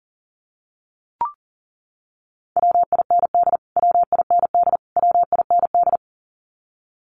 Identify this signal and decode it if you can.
Morse code, heard as a keyed tone, about 40 wpm, 700 Hz. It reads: WIND WIND WIND
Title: Top 500 Words Repeated 40wpm